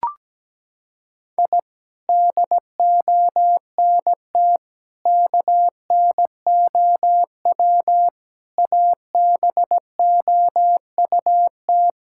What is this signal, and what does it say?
Morse code, heard as a keyed tone, about 17 wpm, 700 Hz. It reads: I DONT KNOW ABOUT
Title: Sentences from Top 1000 Words - Review Part 5 - 17wpm